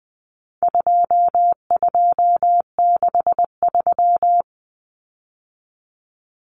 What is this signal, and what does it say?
Morse code, heard as a keyed tone, about 20 wpm, 700 Hz. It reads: 2263
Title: Common Cut Numbers - 4-Digit 20wpm